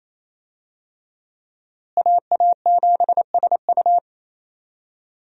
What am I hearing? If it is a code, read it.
Morse code, heard as a keyed tone, about 28 wpm, 700 Hz. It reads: AA7SU
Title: USA Call Signs - 5 Characters - Repeated 3x - 28wpm